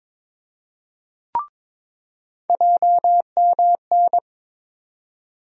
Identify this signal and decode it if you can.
Morse code, heard as a keyed tone, about 22 wpm, 700 Hz. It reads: JMN
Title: USA Call Sign Postfixes - 3 Letters 22wpm